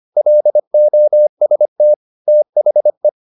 Morse code, heard as a keyed tone, about 25 wpm, 600 Hz. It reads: LOST THE